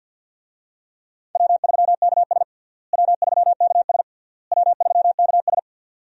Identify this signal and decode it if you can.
Morse code, heard as a keyed tone, about 53 wpm, 700 Hz. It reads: W3KS W3KS W3KS
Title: USA Call Signs - 4 Characters - Repeated 3x - 50wpm